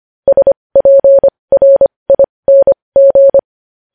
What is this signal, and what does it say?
Morse code, heard as a keyed tone, about 25 wpm, spaced as for 20 wpm, 550 Hz. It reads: SPRING
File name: podcast / News headlines-1x-25x20